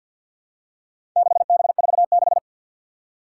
Morse code, heard as a keyed tone, about 52 wpm, 700 Hz. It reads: 6B46
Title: Four Challenging Letter-Number Combinations 50wpm